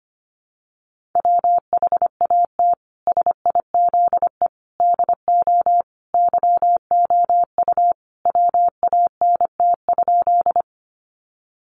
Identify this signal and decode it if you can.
Morse code, heard as a keyed tone, about 25 wpm, 700 Hz. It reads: WHAT SIZE DO YOU WANT?
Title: Sentences from Top 400 Words 25wpm